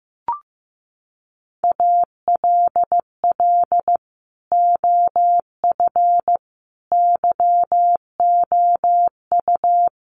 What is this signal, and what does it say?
Morse code, heard as a keyed tone, about 15 wpm, 700 Hz. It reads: ALL OF YOU
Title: Sentences from Top 1200 Words 15wpm